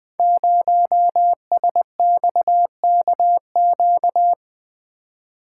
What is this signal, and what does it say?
Morse code, heard as a keyed tone, about 20 wpm, 700 Hz. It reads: 0SXKQ
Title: Code Groups - 5 Letter:Numbers - 20wpm